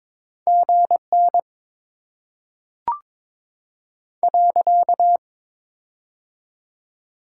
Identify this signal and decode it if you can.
Morse code, heard as a keyed tone, about 22 wpm, 700 Hz. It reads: GN .
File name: Top 90 QSO Elements - 6x Character Spacing - 22wpm